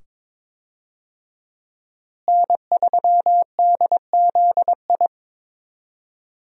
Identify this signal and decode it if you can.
Morse code, heard as a keyed tone, about 22 wpm, 700 Hz. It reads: N3DZI
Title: USA Call Signs - 5 Characters - Repeated 2x - 22wpm